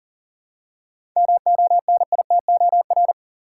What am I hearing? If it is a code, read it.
Morse code, heard as a keyed tone, about 40 wpm, 700 Hz. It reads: MONITOR
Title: Seven Letter Words - 40wpm